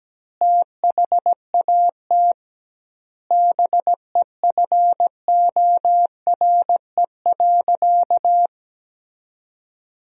Morse code, heard as a keyed tone, about 17 wpm, 700 Hz. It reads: THAT BEFORE.
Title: 2x Word-Spacing - Sentences from Top 400 Words - 17wpm